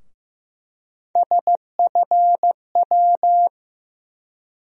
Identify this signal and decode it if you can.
Morse code, heard as a keyed tone, about 15 wpm, 700 Hz. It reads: SFW
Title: USA Call Sign Postfixes - 3 Letters 15wpm